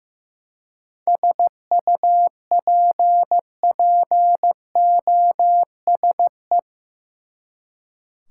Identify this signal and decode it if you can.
Morse code, heard as a keyed tone, about 16 wpm, 700 Hz. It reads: SUPPOSE
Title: Seven Letter Words - 15wpm